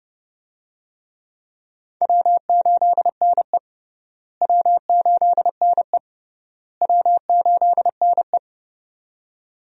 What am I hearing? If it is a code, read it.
Morse code, heard as a keyed tone, about 30 wpm, 700 Hz. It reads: W8NE W8NE W8NE
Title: USA Call Signs - 4 Characters - Repeated 3x - 30wpm